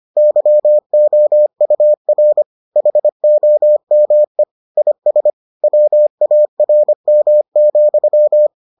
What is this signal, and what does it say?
Morse code, heard as a keyed tone, about 25 wpm, 600 Hz. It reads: YOUR HOME IS WARM,